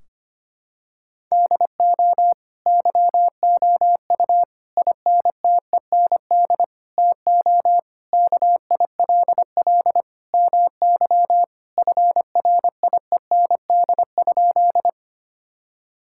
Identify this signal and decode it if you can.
Morse code, heard as a keyed tone, about 25 wpm, 700 Hz. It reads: DO YOU INTEND TO KILL MY FRIEND?
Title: Sentences from Top 1600 Words 25wpm